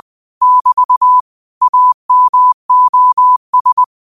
Morse code, heard as a keyed tone, about 20 wpm, 1000 Hz. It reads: <BT> AMOS
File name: qotdfiles / 20wpm